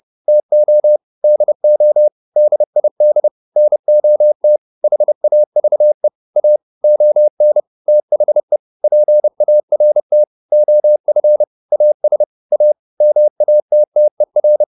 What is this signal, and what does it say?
Morse code, heard as a keyed tone, about 30 wpm, 600 Hz. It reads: TO DO DID NOT HAVE A ON THE PART OF AS A MATTER